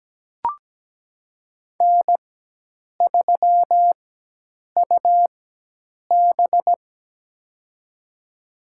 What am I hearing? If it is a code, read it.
Morse code, heard as a keyed tone, about 17 wpm, spaced as for 8 wpm, 700 Hz. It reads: N3UB
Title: USA Call Signs - 4 Characters - 4x Character Spacing - 17wpm